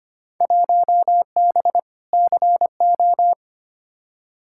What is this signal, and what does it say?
Morse code, heard as a keyed tone, about 25 wpm, 700 Hz. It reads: 1B CO
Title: Field Day Exchanges - 25wpm